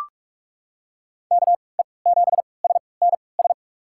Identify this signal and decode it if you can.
Morse code, heard as a keyed tone, about 42 wpm, spaced as for 25 wpm, 700 Hz. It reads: KE7SNS
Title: USA Call Signs - 3x Character Spacing - 45wpm